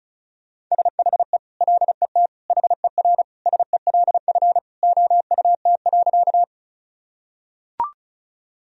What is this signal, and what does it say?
Morse code, heard as a keyed tone, about 36 wpm, 700 Hz. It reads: SHE LET HERSELF OUT.
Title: Sentences from Top 1000 Words - Review Part 1 - 35wpm